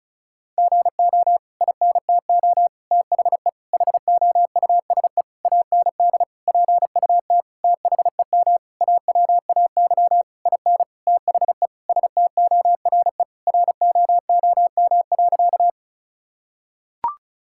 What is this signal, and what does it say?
Morse code, heard as a keyed tone, about 36 wpm, 700 Hz. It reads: GO INTO THE HOUSE AND PUT THEM AWAY IN THE STORE ROOM.
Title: Sentences from Top 1000 Words - Review Part 5 - 35wpm